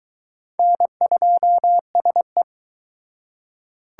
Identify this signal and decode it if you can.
Morse code, heard as a keyed tone, about 22 wpm, 700 Hz. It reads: N2SE